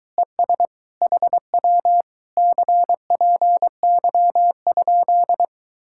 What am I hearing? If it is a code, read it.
Morse code, heard as a keyed tone, about 22 wpm, 700 Hz. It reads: ES HW CPY?